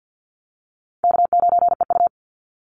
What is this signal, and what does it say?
Morse code, heard as a keyed tone, about 50 wpm, 700 Hz. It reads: K9EU
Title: Easy Call Signs - 50wpm